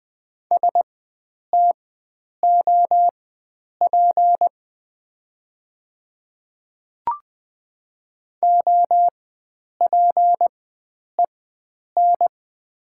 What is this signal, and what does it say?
Morse code, heard as a keyed tone, about 20 wpm, spaced as for 9 wpm, 700 Hz. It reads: STOP OPEN
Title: 4x Character Spacing - Top 300 Words - 20wpm